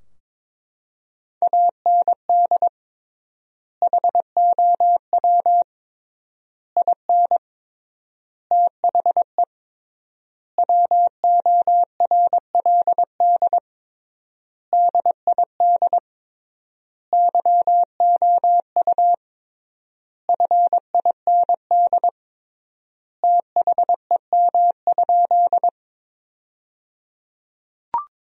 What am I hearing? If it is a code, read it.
Morse code, heard as a keyed tone, about 22 wpm, 700 Hz. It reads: AND HOW IN THE WORLD DID YOU FIND THEM?
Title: 3x Word-Spacing - Sentences from Top 200 Words - 22wpm